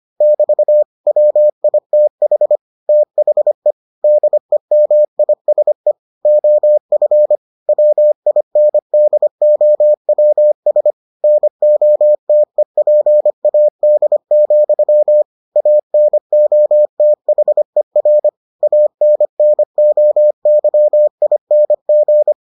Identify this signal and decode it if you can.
Morse code, heard as a keyed tone, about 25 wpm, 600 Hz. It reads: <BT> WITH THE DEMISE OF WINDOWS NOTEPAD, ANOTHER ANNOYING